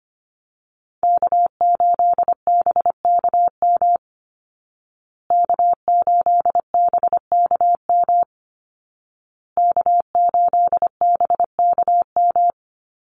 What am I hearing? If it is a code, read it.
Morse code, heard as a keyed tone, about 24 wpm, 700 Hz. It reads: K8BKM K8BKM K8BKM
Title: Call Signs Repeated 2x - 25wpm